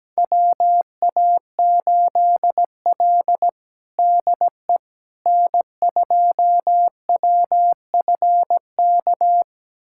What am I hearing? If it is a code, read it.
Morse code, heard as a keyed tone, about 17 wpm, 700 Hz. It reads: WA8L DE N2WFK
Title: USA Callsign DE Callsign 17wpm